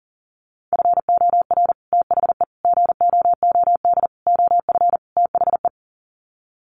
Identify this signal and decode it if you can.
Morse code, heard as a keyed tone, about 40 wpm, 700 Hz. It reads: FOR THE GOOD OF THE
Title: Sets of 5 Words Encore2 40wpm